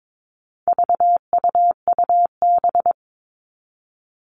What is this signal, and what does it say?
Morse code, heard as a keyed tone, about 22 wpm, 700 Hz. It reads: VUUB
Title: All Cut Numbers - 4-Digit 22wpm